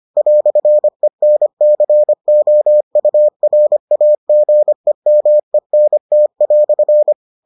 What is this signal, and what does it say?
Morse code, heard as a keyed tone, about 25 wpm, 600 Hz. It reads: "ENCOURAGEMENT"